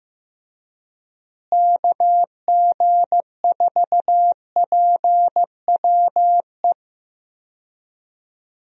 Morse code, heard as a keyed tone, about 15 wpm, 700 Hz. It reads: KG4PWE
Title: USA Call Signs - 6 Characters - Repeated 3x - 15wpm